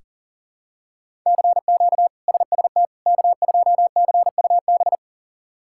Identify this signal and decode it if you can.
Morse code, heard as a keyed tone, about 40 wpm, 700 Hz. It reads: CQ SST K2CUB
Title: K1USN SST QSO Practice - 40wpm